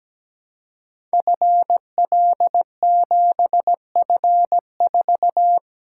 Morse code, heard as a keyed tone, about 17 wpm, 700 Hz. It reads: FL7F4
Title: Code Groups - 5 Letter:Numbers - 17wpm